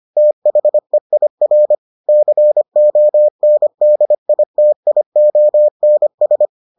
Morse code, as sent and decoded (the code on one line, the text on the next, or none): - .... .. .-. / -.-. --- -. -.. .. - .. --- -. ...
THEIR CONDITIONS